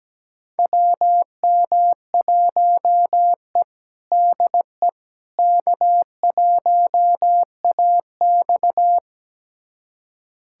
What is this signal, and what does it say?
Morse code, heard as a keyed tone, about 17 wpm, 700 Hz. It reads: WM1E DE K1AX
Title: USA Callsign DE Callsign - 4 Characters - 17wpm